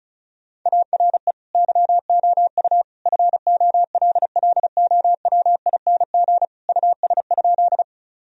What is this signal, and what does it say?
Morse code, heard as a keyed tone, about 36 wpm, 700 Hz. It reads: ARE YOU FOLLOWING US?
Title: Sentences from Top 1000 Words - Review Part 4 - 35wpm